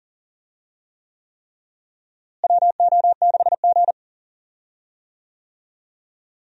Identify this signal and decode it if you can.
Morse code, heard as a keyed tone, about 40 wpm, 700 Hz. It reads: WO6G
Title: USA Call Signs - 4 Characters - Repeated 3x - 40wpm